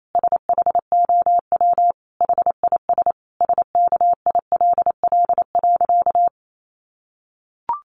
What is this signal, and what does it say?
Morse code, heard as a keyed tone, about 28 wpm, 700 Hz. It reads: SHOW HIS SKILL.
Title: Sentences from Top 1200 Words 28wpm